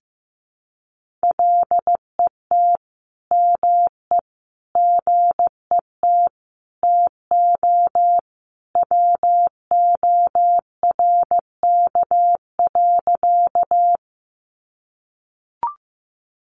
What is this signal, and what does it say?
Morse code, heard as a keyed tone, about 15 wpm, 700 Hz. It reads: LET ME GET TO WORK.
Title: Sentences from Top 300 Words 15wpm